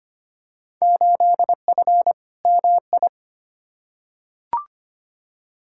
Morse code, heard as a keyed tone, about 25 wpm, 700 Hz. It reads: 8F MI
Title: Field Day Exchanges Encore - 25wpm